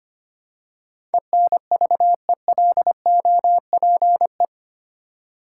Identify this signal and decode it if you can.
Morse code, heard as a keyed tone, about 25 wpm, 700 Hz. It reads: ENVELOPE
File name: practice / 25wpm / Easy Words with Hints - 25wpm